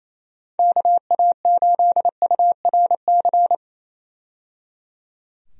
Morse code, heard as a keyed tone, about 28 wpm, 700 Hz. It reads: KA8URC
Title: USA Call Signs - 6 Characters - 28wpm